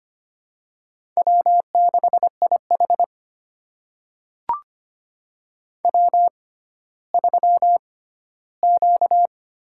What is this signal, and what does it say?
Morse code, heard as a keyed tone, about 25 wpm, spaced as for 14 wpm, 700 Hz. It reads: W6IH W3Q
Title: USA Call Signs - 4 Characters - 6x Character Spacing - 25wpm